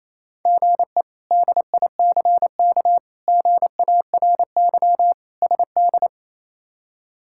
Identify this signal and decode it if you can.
Morse code, heard as a keyed tone, about 28 wpm, 700 Hz. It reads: GE DICK GARY SD
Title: K1USN SST QSO Practice - 28wpm